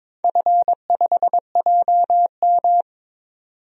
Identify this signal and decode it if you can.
Morse code, heard as a keyed tone, about 22 wpm, 700 Hz. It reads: F5JM
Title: DXCC-100 Call Signs - 4 Characters - 22wpm